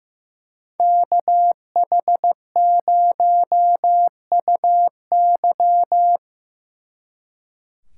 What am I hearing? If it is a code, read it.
Morse code, heard as a keyed tone, about 18 wpm, 700 Hz. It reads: KH0UY
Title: Speed Racing - USA 2x2 Call Signs 15wpm